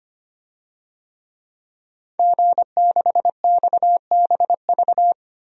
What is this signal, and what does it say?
Morse code, heard as a keyed tone, about 25 wpm, 700 Hz. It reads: G6XBV
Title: DXCC-100 Call Signs - 5 Characters - 25wpm